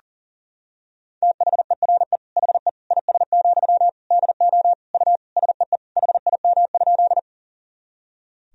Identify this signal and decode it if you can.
Morse code, heard as a keyed tone, about 40 wpm, 700 Hz. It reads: THERE HE IS, DO U SEE HIM?
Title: Sentences from Top 100 Words and Abbreviations 40wpm